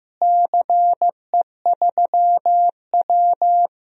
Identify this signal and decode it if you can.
Morse code, heard as a keyed tone, about 15 wpm, 700 Hz. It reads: CE3W
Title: DXCC-100 Call Signs - 4 Characters - Repeated 3x - 15wpm